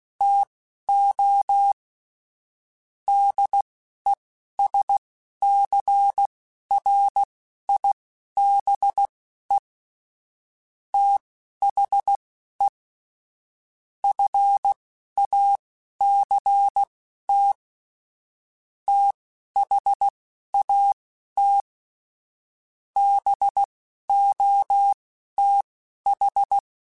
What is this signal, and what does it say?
Morse code, heard as a keyed tone, about 16 wpm, spaced as for 11 wpm, 800 Hz. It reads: TO DESCRIBE THE FACT THAT BOTH